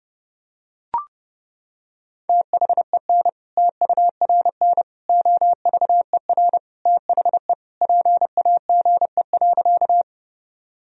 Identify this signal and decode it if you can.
Morse code, heard as a keyed tone, about 30 wpm, 700 Hz. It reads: THEN TURN OVER THE PAGE.
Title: Sentences from Top 1000 Words - Review Part 1 - 30wpm